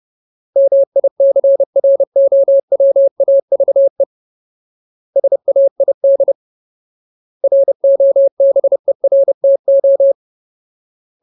Morse code, heard as a keyed tone, about 30 wpm, 550 Hz. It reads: MICROWAVE SAID ROBERTO
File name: Ultimate Word List / 30WPM Ultimate